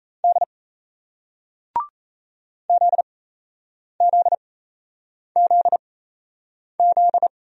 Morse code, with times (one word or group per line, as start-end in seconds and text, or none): 0.2-0.4 D
2.7-7.3 ZZZZ